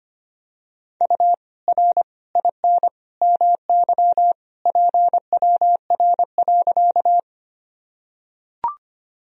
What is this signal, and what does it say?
Morse code, heard as a keyed tone, about 25 wpm, 700 Hz. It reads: U R IN MY PWR.
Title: Sentences from Top 500 Words and Abbreviations 25wpm